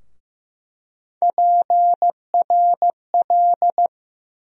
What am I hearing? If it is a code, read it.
Morse code, heard as a keyed tone, about 15 wpm, 700 Hz. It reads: PRL
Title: USA Call Sign Postfixes - 3 Letters 15wpm